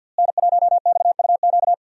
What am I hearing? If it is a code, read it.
Morse code, heard as a keyed tone, about 45 wpm, 700 Hz. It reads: N1XUQ
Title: USA Call Signs - 5 Characters - Repeated 3x - 50wpm